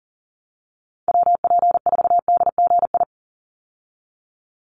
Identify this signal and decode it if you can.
Morse code, heard as a keyed tone, about 40 wpm, 700 Hz. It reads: WP4DGI
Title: USA Call Signs 40wpm